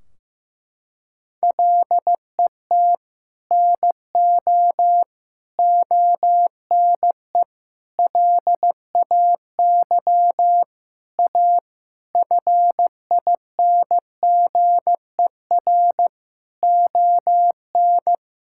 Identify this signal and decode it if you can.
Morse code, heard as a keyed tone, about 15 wpm, 700 Hz. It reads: LET NO ONE LAY A FINGER ON
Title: Sentences from Top 1600 Words 15wpm